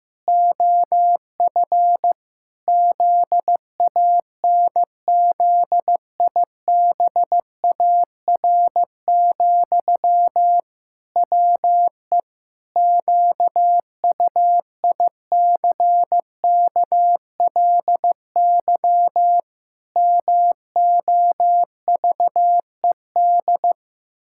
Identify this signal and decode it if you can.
Morse code, heard as a keyed tone, about 15 wpm, 700 Hz. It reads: OF ZANZIBAR, WE QUICKLY MOVED
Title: Pangram Challenge v2 - 15wpm